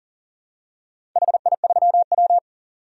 Morse code, heard as a keyed tone, about 39 wpm, 700 Hz. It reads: HI3W
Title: DXCC-100 Call Signs - 4 Characters - 40wpm